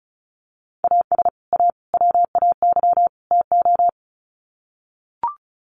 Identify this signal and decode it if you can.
Morse code, heard as a keyed tone, about 34 wpm, 700 Hz. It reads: AS A WAY TO
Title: Sets of 4 Words 35wpm